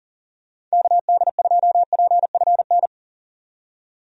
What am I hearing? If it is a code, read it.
Morse code, heard as a keyed tone, about 40 wpm, 700 Hz. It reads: KD2PFN
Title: USA Call Signs - 6 Characters - 40wpm